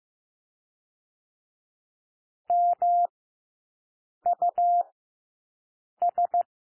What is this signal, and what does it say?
Morse code, heard as a keyed tone, about 15 wpm, 700 Hz. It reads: M U S
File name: cwmp3 / W1 SOUND WORDS 15 wpm